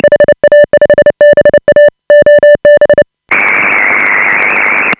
Morse code, heard as a keyed tone, about 29 wpm, 590 Hz. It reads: HA5BA OB
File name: AUDIO2018 / HA5BA CW